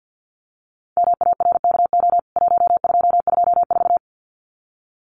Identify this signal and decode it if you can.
Morse code, heard as a keyed tone, about 50 wpm, 700 Hz. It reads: MARKO 1224
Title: CWT Exchange 50wpm